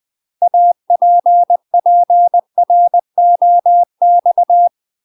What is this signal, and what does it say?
Morse code, heard as a keyed tone, about 20 wpm, 700 Hz. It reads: APPROX
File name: UK MASTER QSOs TRAINING/20WPM